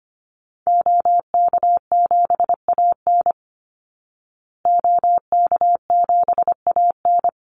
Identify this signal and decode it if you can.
Morse code, heard as a keyed tone, about 25 wpm, 700 Hz. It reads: OK7AN OK7AN
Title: Call Signs Repeated 2x - 25wpm